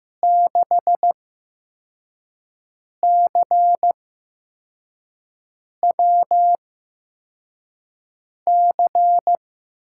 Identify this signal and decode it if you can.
Morse code, heard as a keyed tone, about 15 wpm, spaced as for 4 wpm, 700 Hz. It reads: 6CWC
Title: USA Call Signs - 6 Characters - 8x Character Spacing - 15wpm